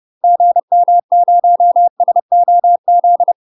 Morse code, heard as a keyed tone, about 30 wpm, 700 Hz. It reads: GM0SOZ
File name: UK MASTER QSOs TRAINING / 30WPM / UK MASTER QSOs 0007